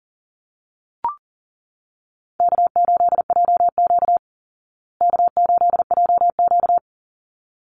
Morse code, heard as a keyed tone, about 40 wpm, 700 Hz. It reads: K8JQ K8JQ
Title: Call Signs Repeated 3x - 40wpm